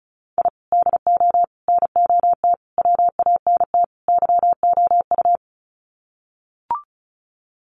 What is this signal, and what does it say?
Morse code, heard as a keyed tone, about 34 wpm, 700 Hz. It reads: I DO NOT WANT YOU
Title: Sets of 5 Words 35wpm